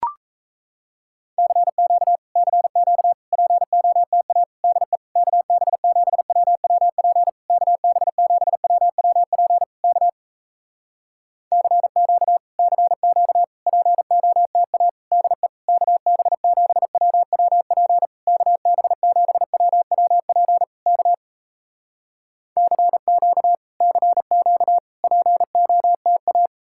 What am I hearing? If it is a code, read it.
Morse code, heard as a keyed tone, about 38 wpm, 700 Hz. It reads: CQ CQ POTA DE KB7WWP KB7WWP K CQ CQ POTA DE KB7WWP KB7WWP K CQ CQ POTA